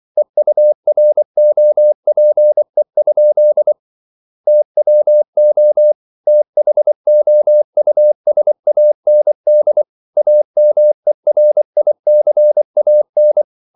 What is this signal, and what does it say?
Morse code, heard as a keyed tone, about 24 wpm, 600 Hz. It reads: EUROPE? TWO THOUSAND AMERICAN